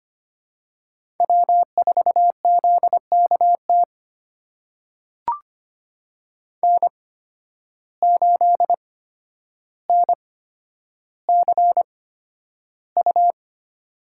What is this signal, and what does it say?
Morse code, heard as a keyed tone, about 25 wpm, spaced as for 11 wpm, 700 Hz. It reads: W4ZKT N8NCU
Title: USA Call Signs - 5 Characters - 8x Character Spacing - 25wpm